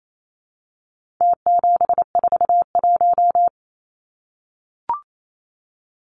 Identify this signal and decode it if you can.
Morse code, heard as a keyed tone, about 28 wpm, 700 Hz. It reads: T741
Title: Numbers - Contest Style - 4-Digit 28wpm